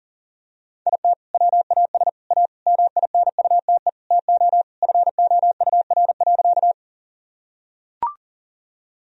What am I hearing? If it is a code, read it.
Morse code, heard as a keyed tone, about 40 wpm, 700 Hz. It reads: IT WAS A MINUTE TO FOUR.
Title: Sentences from Top 1000 Words - Review Part 1 - 40wpm